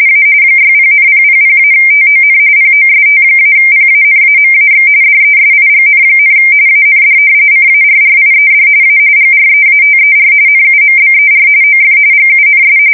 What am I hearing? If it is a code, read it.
RTTY, about 45.45 baud. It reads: A 60,000-TON POST-PANAMAX DESIGN, AND CARRIED A GREATER NUMBER OF GUNS
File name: fsk170 / battleships